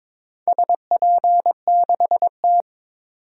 Morse code, heard as a keyed tone, about 23 wpm, 700 Hz. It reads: SP6T
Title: DXCC-100 Call Signs - 4 Characters - 22wpm